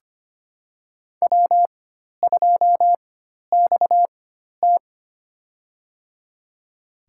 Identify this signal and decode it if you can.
Morse code, heard as a keyed tone, about 25 wpm, spaced as for 12 wpm, 700 Hz. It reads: W2XT
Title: USA Call Signs - 4x Character Spacing - 25wpm